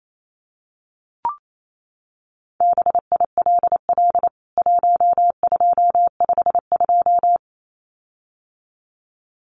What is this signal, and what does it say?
Morse code, heard as a keyed tone, about 28 wpm, 700 Hz. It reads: BILL 1252
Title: CWT Exchange 28wpm